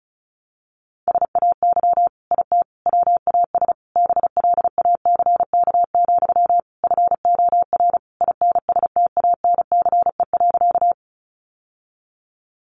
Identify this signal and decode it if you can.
Morse code, heard as a keyed tone, about 34 wpm, 700 Hz. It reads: SAY IT WAS BLACK, FOR INSTANCE.
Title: Sentences from Top 800 Words 35wpm